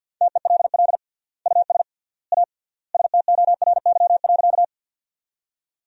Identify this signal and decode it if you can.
Morse code, heard as a keyed tone, about 53 wpm, 700 Hz. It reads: TELL US A STORY.
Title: 3x Word-Spacing - Sentences from Top 300 Words - 50wpm